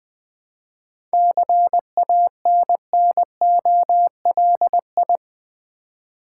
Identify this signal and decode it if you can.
Morse code, heard as a keyed tone, about 21 wpm, 700 Hz. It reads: CANNOLI
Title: Difficult Words with Hints - 20wpm